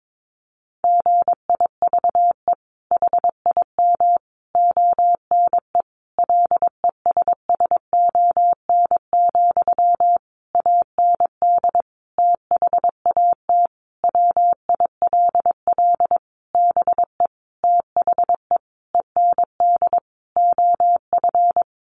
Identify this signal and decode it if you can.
Morse code, heard as a keyed tone, about 22 wpm, 700 Hz. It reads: GIVE HIM ONE LESSON, AND THAT WILL BE THE END OF
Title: Sentences from Top 1200 Words 22wpm